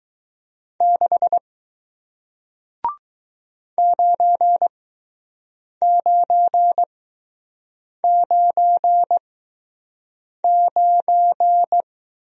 Morse code, heard as a keyed tone, about 20 wpm, spaced as for 6 wpm, 700 Hz. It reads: 6 9999